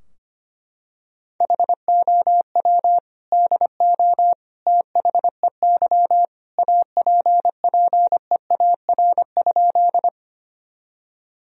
Morse code, heard as a keyed tone, about 25 wpm, 700 Hz. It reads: HOW DO THEY APPEAR?
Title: Sentences from Top 1000 Words - Review Part 2 - 25wpm